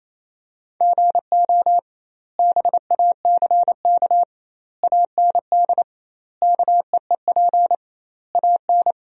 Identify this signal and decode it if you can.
Morse code, heard as a keyed tone, about 28 wpm, 700 Hz. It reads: GO BACK AND KEEP AN
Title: 2x Word-Spacing - Sentences from Top 300 Words - 28wpm